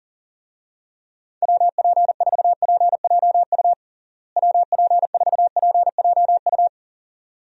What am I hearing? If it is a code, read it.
Morse code, heard as a keyed tone, about 40 wpm, 700 Hz. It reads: WP4PJU WP4PJU
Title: USA Call Signs - 6 Characters - Repeated 2x - 40wpm